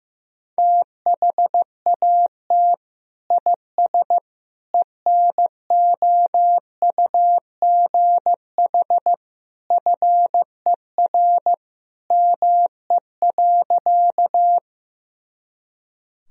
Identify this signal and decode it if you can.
Morse code, heard as a keyed tone, about 15 wpm, 700 Hz. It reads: THAT IS ENOUGH FER ME.
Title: Sentences from Top 400 Words and Abbreviations 15wpm